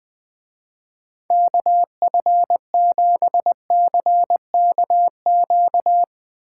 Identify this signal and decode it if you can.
Morse code, heard as a keyed tone, about 19 wpm, 700 Hz. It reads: KF7CKQ
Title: Speed Racing - USA 2x3 Call Signs 17wpm